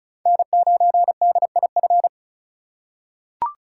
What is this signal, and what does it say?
Morse code, heard as a keyed tone, about 36 wpm, 700 Hz. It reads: N9DIF
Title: USA Call Signs - 5 Characters - 3x Character Spacing - 35wpm